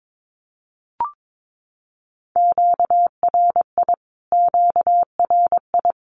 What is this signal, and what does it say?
Morse code, heard as a keyed tone, about 22 wpm, 700 Hz. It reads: QRI QRI
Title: Q-Codes Repeated 3x 22wpm